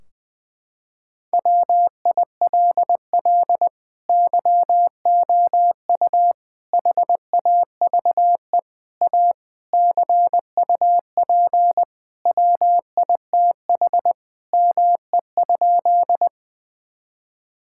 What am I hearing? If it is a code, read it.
Morse code, heard as a keyed tone, about 20 wpm, 700 Hz. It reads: WILL YOU HAVE A CUP WITH ME?
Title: Sentences from Top 1600 Words 20wpm